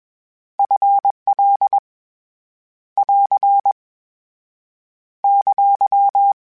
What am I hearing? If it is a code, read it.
Morse code, heard as a keyed tone, about 20 wpm, 800 Hz. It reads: F L <AR> !
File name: Contrasts / Contrasts-mp3s / 2LtrContrasts @20